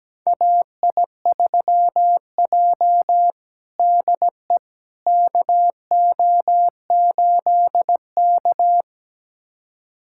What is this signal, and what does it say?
Morse code, heard as a keyed tone, about 17 wpm, 700 Hz. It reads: AI3J DE KO8K
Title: USA Callsign DE Callsign - 4 Characters - 17wpm